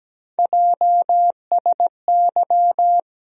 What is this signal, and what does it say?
Morse code, heard as a keyed tone, about 17 wpm, 700 Hz. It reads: JSY
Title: USA Call Sign Postfixes - 3 Letters 17wpm